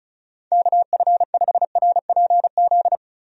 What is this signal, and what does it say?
Morse code, heard as a keyed tone, about 36 wpm, 700 Hz. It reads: KF5RPZ
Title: USA Call Signs - 6 Characters - Repeated 2x - 35wpm